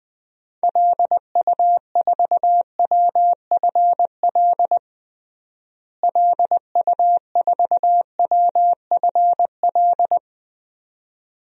Morse code, heard as a keyed tone, about 21 wpm, 700 Hz. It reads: LU4WFL LU4WFL
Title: DXCC-100 Call Signs - 6 Characters - Repeated 2x - 20wpm